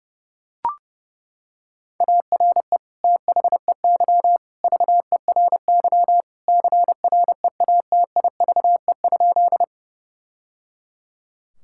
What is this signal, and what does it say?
Morse code, heard as a keyed tone, about 30 wpm, 700 Hz. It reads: ARE THEY VERY CREATIVE?
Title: Sentences from Top 1000 Words - Review Part 5 - 30wpm